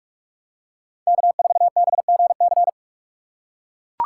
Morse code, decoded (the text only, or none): K4BGC